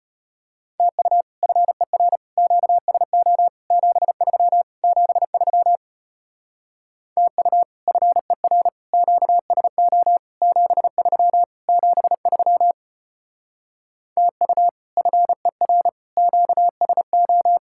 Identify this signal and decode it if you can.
Morse code, heard as a keyed tone, about 33 wpm, 700 Hz. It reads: TU FER QSO 73 73 TU FER QSO 73 73 TU FER QSO